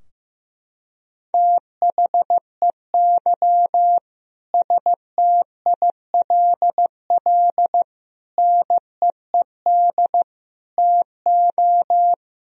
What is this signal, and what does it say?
Morse code, heard as a keyed tone, about 15 wpm, 700 Hz. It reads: THEY STILL NEED TO
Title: Sentences from Top 1000 Words - Review Part 4 - 15wpm